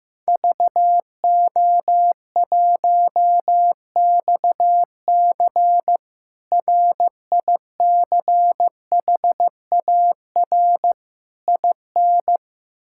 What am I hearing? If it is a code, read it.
Morse code, heard as a keyed tone, about 15 wpm, 700 Hz. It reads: VO1XC RICHAR IN
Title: K1USN SST QSO Practice - 15wpm